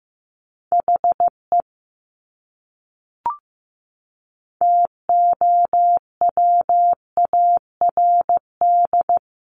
Morse code, heard as a keyed tone, about 15 wpm, 700 Hz. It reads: HE TOWARD
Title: Top 500 Words Repeated 15wpm